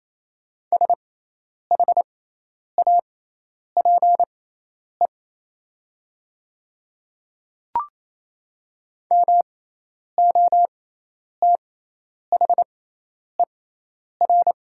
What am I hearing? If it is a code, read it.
Morse code, heard as a keyed tone, about 28 wpm, spaced as for 10 wpm, 700 Hz. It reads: SHAPE MOTHER
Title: 6x Character Spacing - Top 500 Words - 28wpm